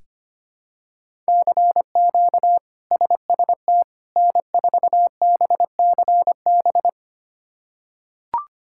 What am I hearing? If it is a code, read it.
Morse code, heard as a keyed tone, about 25 wpm, 700 Hz. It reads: CQ SST N4BCB
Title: K1USN SST QSO Practice - 25wpm